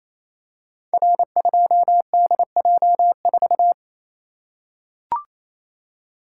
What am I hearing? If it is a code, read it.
Morse code, heard as a keyed tone, about 29 wpm, 700 Hz. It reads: R2DJ4
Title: Code Groups - 5 Letter:Numbers - 28wpm